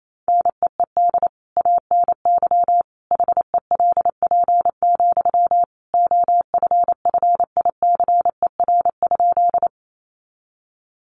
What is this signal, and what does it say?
Morse code, heard as a keyed tone, about 28 wpm, 700 Hz. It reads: NEED ANY HELP, OFFICER?
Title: Sentences from Top 1200 Words 28wpm